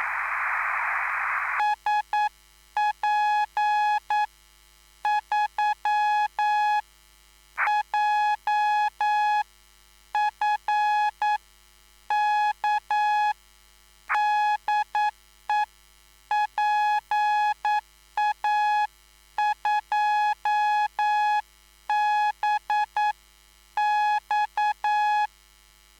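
Morse code, heard as a keyed tone, about 9 wpm, spaced as for 8 wpm, 840 Hz. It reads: SP3JFK DE PA2BX
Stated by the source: Live CW QSO between PA2BX and SP3JFK on the 14055 kHz QRS frequency, at a basic 8 wpm.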